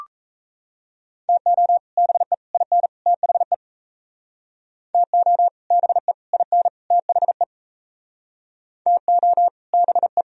Morse code, heard as a keyed tone, about 37 wpm, 700 Hz. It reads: TO BE IN THE TO BE IN THE TO BE